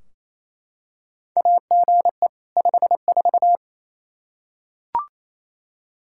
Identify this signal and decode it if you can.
Morse code, heard as a keyed tone, about 28 wpm, 700 Hz. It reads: AGE 54
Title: Practice QSO - Exchange 3 - More Info - 28wpm